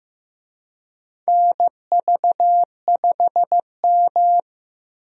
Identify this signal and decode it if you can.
Morse code, heard as a keyed tone, about 16 wpm, 700 Hz. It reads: NV5M
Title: USA Call Signs - 4 Characters - 15wpm